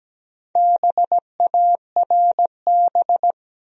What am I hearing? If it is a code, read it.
Morse code, heard as a keyed tone, about 17 wpm, 700 Hz. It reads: BARB
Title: USA Common Names - Max 4 Letters - 17wpm